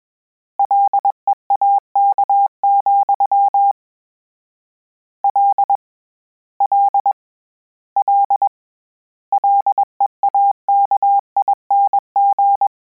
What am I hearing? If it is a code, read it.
Morse code, heard as a keyed tone, about 21 wpm, 800 Hz. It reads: LEAK, L L L LEAKING
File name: Combinations / Combinations-mp3s / Combinations@20